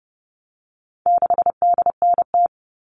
Morse code, heard as a keyed tone, about 30 wpm, 700 Hz. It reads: 6DNT
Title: All Cut Numbers - 4-Digit 30wpm